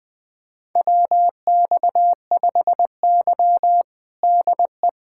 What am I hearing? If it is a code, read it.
Morse code, heard as a keyed tone, about 19 wpm, 700 Hz. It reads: WX5Y DE
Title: USA Callsign DE Callsign 20wpm